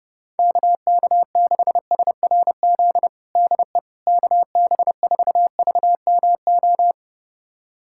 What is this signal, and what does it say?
Morse code, heard as a keyed tone, about 30 wpm, 700 Hz. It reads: KK6SRZ DE KB4VMO
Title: USA Callsign DE Callsign - 6 Characters - 30wpm